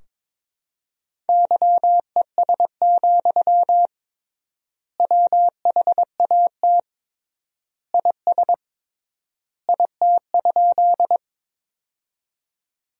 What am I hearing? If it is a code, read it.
Morse code, heard as a keyed tone, about 22 wpm, 700 Hz. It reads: YES, WHAT IS IT?
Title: 3x Word-Spacing - Sentences from Top 500 Words - 22wpm